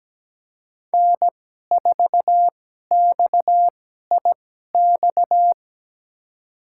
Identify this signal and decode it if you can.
Morse code, heard as a keyed tone, about 17 wpm, spaced as for 12 wpm, 700 Hz. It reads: N4XIX
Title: USA Call Signs - 5 Characters - 2x Character Spacing - 17wpm